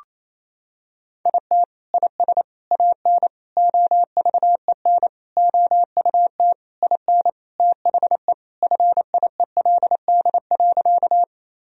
Morse code, heard as a keyed tone, about 28 wpm, 700 Hz. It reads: IT IS AN OVEN OUT IN THE FIELD.
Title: Sentences from Top 1000 Words - Review Part 4 - 28wpm